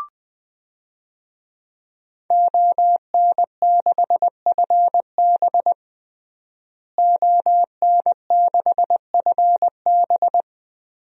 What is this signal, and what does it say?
Morse code, heard as a keyed tone, about 20 wpm, 700 Hz. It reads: ON6FB ON6FB
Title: DXCC-100 Call Signs - 5 Characters - Repeated 3x - 20wpm